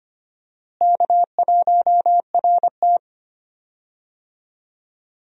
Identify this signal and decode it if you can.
Morse code, heard as a keyed tone, about 25 wpm, 700 Hz. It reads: K1RT
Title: USA Call Signs - 4 Characters - Repeated 3x - 25wpm